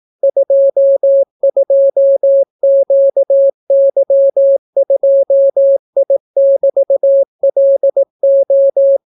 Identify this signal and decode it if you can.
Morse code, heard as a keyed tone, about 18 wpm, 550 Hz. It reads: 22QY2I<BT>LO